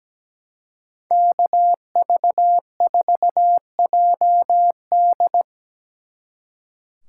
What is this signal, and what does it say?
Morse code, heard as a keyed tone, about 20 wpm, 700 Hz. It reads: KV4JD
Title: Speed Racing - USA 2x2 Call Signs 17wpm